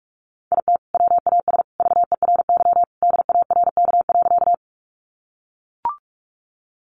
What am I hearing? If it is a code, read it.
Morse code, heard as a keyed tone, about 44 wpm, 700 Hz. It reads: IT WAS VERY DARK.
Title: Sentences from Top 500 Words 45wpm